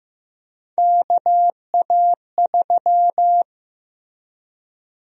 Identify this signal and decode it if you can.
Morse code, heard as a keyed tone, about 15 wpm, 700 Hz. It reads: KA3
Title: USA Call Sign Prefixes - Repeated 2x - 15wpm